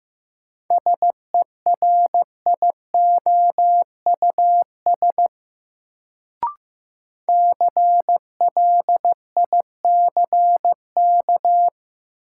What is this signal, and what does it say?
Morse code, heard as a keyed tone, about 15 wpm, 700 Hz. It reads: SERIOUS CLICK
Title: Sentences from Top 1100 Words - New Word Drill - 15wpm